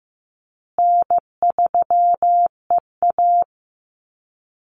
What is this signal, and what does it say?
Morse code, heard as a keyed tone, about 15 wpm, 700 Hz. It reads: N3EA
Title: Easy Call Signs Repeated 3x - 15wpm